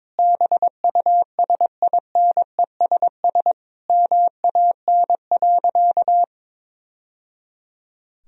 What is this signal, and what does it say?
Morse code, heard as a keyed tone, about 21 wpm, 700 Hz. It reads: BUSINESS MAN.
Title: Sentences from Top 1000 Words - Review Part 4 - 22wpm